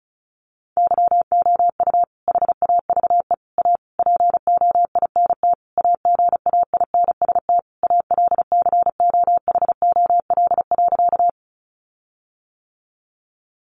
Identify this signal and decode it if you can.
Morse code, heard as a keyed tone, about 34 wpm, 700 Hz. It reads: YOU HAVE A POINT AGAINST ALCOHOL.
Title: Sentences from Top 1400 Words 35wpm